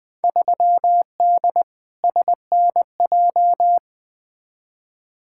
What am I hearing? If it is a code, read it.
Morse code, heard as a keyed tone, about 21 wpm, 700 Hz. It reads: 3D SNJ
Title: Field Day Exchanges Encore - 20wpm